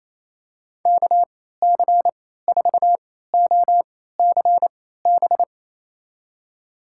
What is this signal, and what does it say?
Morse code, heard as a keyed tone, about 28 wpm, spaced as for 16 wpm, 700 Hz. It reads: KC4OCB
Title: USA Call Signs - 6 Characters - 3x Character Spacing - 28wpm